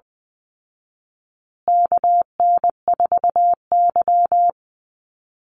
Morse code, heard as a keyed tone, about 20 wpm, 700 Hz. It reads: KN4Y
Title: Call Signs Repeated 2x - 20wpm